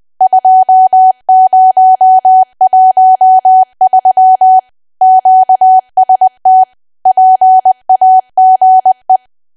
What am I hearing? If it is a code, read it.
Morse code, heard as a keyed tone, about 20 wpm, 750 Hz. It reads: 2013 QST PAGE
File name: cwa / W1AW-Level3 / 20WPM